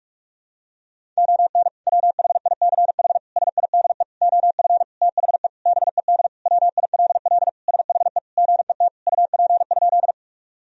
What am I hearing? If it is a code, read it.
Morse code, heard as a keyed tone, about 46 wpm, 700 Hz. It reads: ON WHICH SIDE OF THE BED WILL SHE GET UP?
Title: Sentences from Top 1000 Words - Review Part 1 - 45wpm